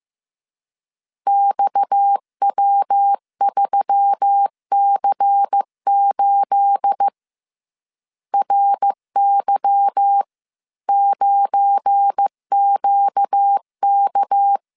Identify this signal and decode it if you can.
Morse code, heard as a keyed tone, about 15 wpm, 800 Hz. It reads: XW3C8 RY9QK
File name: CW / DET / 1 9 1500C S C 14wpm 3245